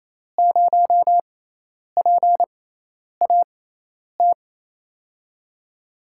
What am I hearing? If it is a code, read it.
Morse code, heard as a keyed tone, about 28 wpm, spaced as for 10 wpm, 700 Hz. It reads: 0PAT
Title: USA Call Signs - 6 Characters - 6x Character Spacing - 28wpm